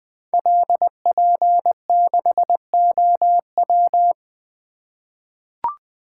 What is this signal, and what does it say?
Morse code, heard as a keyed tone, about 20 wpm, 700 Hz. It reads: LP6OW
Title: Code Groups - 5 Letter:Numbers - 20wpm